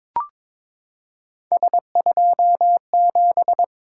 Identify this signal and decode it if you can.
Morse code, heard as a keyed tone, about 22 wpm, 700 Hz. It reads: S27
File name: practice / 22wpm / Three Letter-Number Combinations - 22wpm